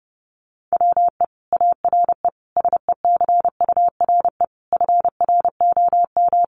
Morse code, heard as a keyed tone, about 30 wpm, 700 Hz. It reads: WE ARE SECURE FROM
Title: Sentences from Top 1200 Words 30wpm